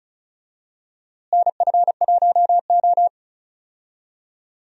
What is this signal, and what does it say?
Morse code, heard as a keyed tone, about 36 wpm, 700 Hz. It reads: NF1O
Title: USA Call Signs - 4 Characters - Repeated 2x - 35wpm